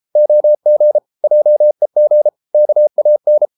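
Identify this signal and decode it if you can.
Morse code, heard as a keyed tone, about 33 wpm, 600 Hz. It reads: OG JEG KAN
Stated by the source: Morse code at 33 wpm, QRQ.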